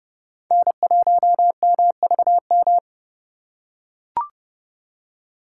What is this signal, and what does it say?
Morse code, heard as a keyed tone, about 30 wpm, 700 Hz. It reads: N1MVM
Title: USA Call Signs - 2x Character Spacing - 30wpm